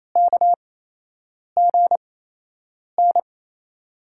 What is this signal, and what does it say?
Morse code, heard as a keyed tone, about 28 wpm, spaced as for 8 wpm, 700 Hz. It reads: KGN
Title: USA Call Signs - 6 Characters - 8x Character Spacing - 28wpm